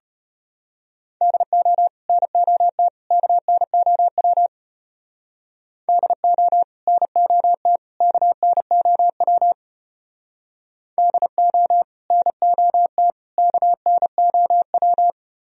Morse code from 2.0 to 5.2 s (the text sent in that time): NOT KNOW